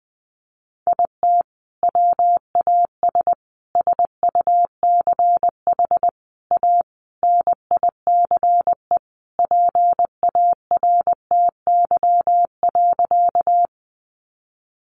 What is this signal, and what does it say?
Morse code, heard as a keyed tone, about 20 wpm, 700 Hz. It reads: IT WAS SUCH A NICE PARTY.
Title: Sentences from Top 1400 Words 20wpm